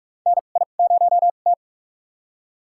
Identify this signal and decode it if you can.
Morse code, heard as a keyed tone, about 45 wpm, spaced as for 32 wpm, 700 Hz. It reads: NI0T
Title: USA Call Signs - 4 Characters - 2x Character Spacing - 45wpm